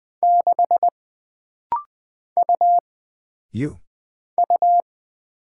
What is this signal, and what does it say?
Morse code, heard as a keyed tone, about 20 wpm, 700 Hz. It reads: 6 U U
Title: Single Character (Letter:Number:Punctuation:Prosign) - Rapid-Fire - 20wpm